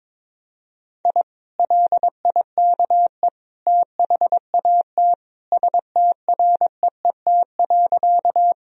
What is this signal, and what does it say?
Morse code, heard as a keyed tone, about 22 wpm, 700 Hz. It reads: I LIKE THAT STREET.
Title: Sentences from Top 1000 Words - Review Part 3 - 22wpm